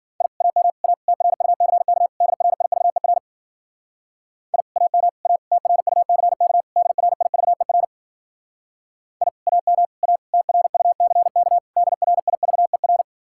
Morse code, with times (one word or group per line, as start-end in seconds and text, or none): 0.2-0.3 I
0.4-0.7 AM
0.8-0.9 A
1.1-2.1 TRUCK
2.2-3.2 DRIVER
4.5-4.6 I
4.8-5.1 AM
5.3-5.4 A
5.5-6.6 TRUCK
6.8-7.8 DRIVER
9.2-9.3 I
9.5-9.9 AM
10.0-10.2 A
10.3-11.6 TRUCK
11.8-13.0 DRIVER